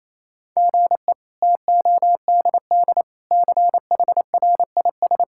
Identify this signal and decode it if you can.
Morse code, heard as a keyed tone, about 27 wpm, 700 Hz. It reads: GE TODD CHRIS
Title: K1USN SST QSO Practice - 28wpm